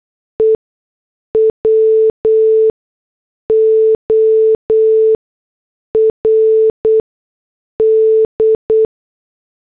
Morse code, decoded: E WORD